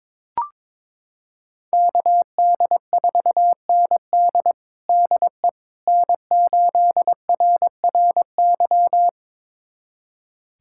Morse code, heard as a keyed tone, about 22 wpm, 700 Hz. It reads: KD4ND DE N8RRY
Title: USA Callsign DE Callsign - 5 Characters - 22wpm